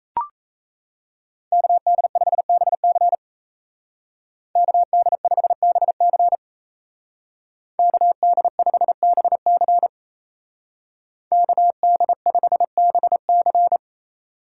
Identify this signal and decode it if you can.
Morse code, heard as a keyed tone, about 33 wpm, 700 Hz. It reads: KD5BC KD5BC KD5BC KD5BC
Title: Speed Racing - USA 2x2 Call Signs 28wpm